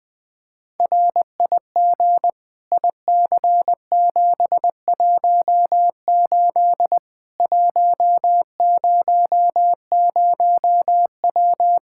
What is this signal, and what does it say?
Morse code, heard as a keyed tone, about 20 wpm, 700 Hz. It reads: RIG IC718 100W
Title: Practice QSO - Exchange 2 - Info - 20wpm